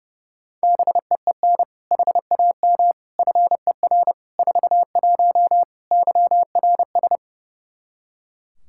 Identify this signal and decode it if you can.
Morse code, heard as a keyed tone, about 30 wpm, 700 Hz. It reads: BEEN HAM FER 41 YRS
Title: K1USN SST QSO Practice 30wpm